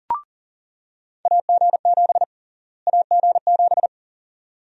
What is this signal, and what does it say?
Morse code, heard as a keyed tone, about 37 wpm, 700 Hz. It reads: AG7 AG7
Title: USA Call Sign Prefixes - Repeated 2x - 40wpm